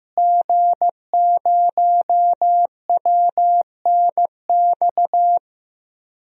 Morse code, heard as a keyed tone, about 15 wpm, 700 Hz. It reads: G0WNX
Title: DXCC-100 Call Signs - 5 Characters - Repeated 3x - 15wpm